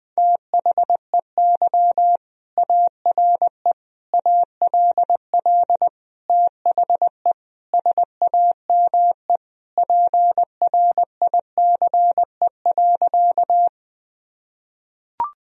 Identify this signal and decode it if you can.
Morse code, heard as a keyed tone, about 21 wpm, 700 Hz. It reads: THEY ARE ALL THE SAME PRICE.
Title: Sentences from Top 1000 Words - Review Part 4 - 20wpm